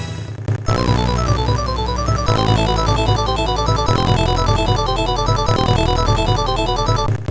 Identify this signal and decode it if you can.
There is no code.